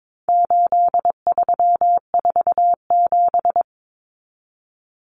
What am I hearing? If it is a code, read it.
Morse code, heard as a keyed tone, about 22 wpm, 700 Hz. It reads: 8347
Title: Numbers - 4-Digit 22wpm